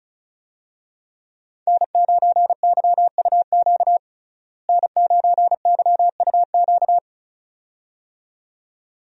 Morse code, heard as a keyed tone, about 34 wpm, 700 Hz. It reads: N9YUQ N9YUQ
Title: USA Call Signs - 5 Characters - Repeated 2x - 35wpm